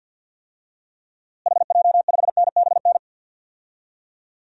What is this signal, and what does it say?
Morse code, heard as a keyed tone, about 52 wpm, 700 Hz. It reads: HJ5NBN
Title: DXCC-100 Call Signs - 6 Characters - Repeated 2x - 50wpm